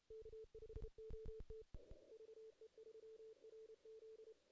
Morse code, heard as a keyed tone, about 32 wpm, 440 Hz. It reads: K5OT VE3RZ
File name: 2016 JAN CW NAQP / 2016 Jan naqp k5ot - 0-8